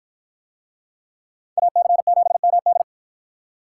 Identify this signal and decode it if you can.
Morse code, heard as a keyed tone, about 48 wpm, 700 Hz. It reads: AC7MD